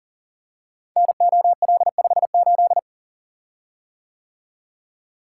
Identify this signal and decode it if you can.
Morse code, heard as a keyed tone, about 40 wpm, 700 Hz. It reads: NOL58
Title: Code Groups - 5 Letter:Numbers - 40wpm